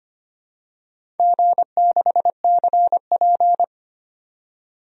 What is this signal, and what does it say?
Morse code, heard as a keyed tone, about 25 wpm, 700 Hz. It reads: G6CP
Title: DXCC-100 Call Signs - 4 Characters - 25wpm